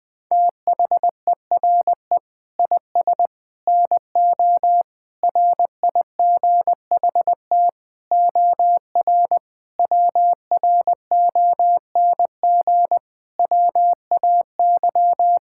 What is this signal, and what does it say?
Morse code, heard as a keyed tone, about 21 wpm, 700 Hz. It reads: THERE IS NO RIGHT OR WRONG WAY
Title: Sets of 8 Words - 20wpm